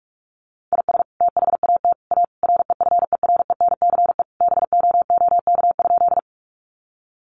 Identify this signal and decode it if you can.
Morse code, heard as a keyed tone, about 44 wpm, 700 Hz. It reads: IS THAT A REFERENCE BOOK?
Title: Sentences from Top 1200 Words 45wpm